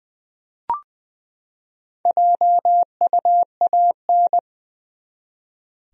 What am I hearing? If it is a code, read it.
Morse code, heard as a keyed tone, about 20 wpm, 700 Hz. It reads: JUAN
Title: USA Common Names - 20wpm